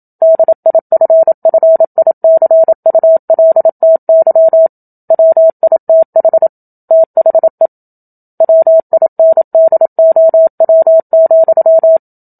Morse code, heard as a keyed tone, about 27 wpm, 650 Hz. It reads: DIFFICULTY WITH THE WINDOW,
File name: Book Two / 01-Chapter 1 - 27 WPM